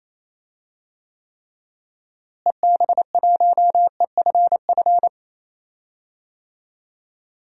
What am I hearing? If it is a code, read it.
Morse code, heard as a keyed tone, about 27 wpm, 700 Hz. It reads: EB1EFF
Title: DXCC-100 Call Signs - 6 Characters - Repeated 3x - 28wpm